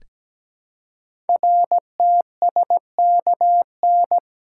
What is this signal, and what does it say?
Morse code, heard as a keyed tone, about 17 wpm, 700 Hz. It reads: RTSKN
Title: Code Groups - 5 Letters - 17wpm